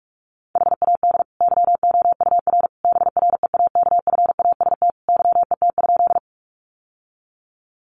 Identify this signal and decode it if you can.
Morse code, heard as a keyed tone, about 44 wpm, 700 Hz. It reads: HAD YOUR BREAKFAST YET?
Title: Sentences from Top 1500 Words 45wpm